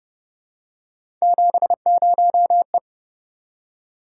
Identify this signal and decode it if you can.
Morse code, heard as a keyed tone, about 30 wpm, 700 Hz. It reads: 70E
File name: Speed-Racing - Three Letter-Number Combinations - 25wpm